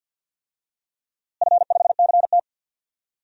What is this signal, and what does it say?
Morse code, heard as a keyed tone, about 50 wpm, 700 Hz. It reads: F5CT
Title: DXCC-100 Call Signs - 4 Characters - Repeated 2x - 50wpm